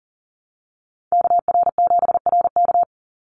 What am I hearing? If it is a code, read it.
Morse code, heard as a keyed tone, about 40 wpm, 700 Hz. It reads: KR7RK
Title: NA SOTA Activator Callsigns 40wpm